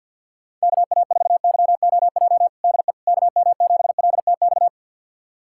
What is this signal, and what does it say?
Morse code, heard as a keyed tone, about 50 wpm, 700 Hz. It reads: KA4YOJ DE KM7LTX
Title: USA Callsign DE Callsign 50wpm